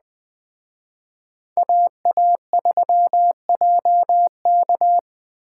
Morse code, heard as a keyed tone, about 21 wpm, 700 Hz. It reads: AA3JK